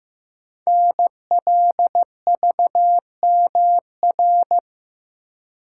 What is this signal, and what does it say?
Morse code, heard as a keyed tone, about 15 wpm, 700 Hz. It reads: NLVMR
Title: Code Groups - 5 Letters - 15wpm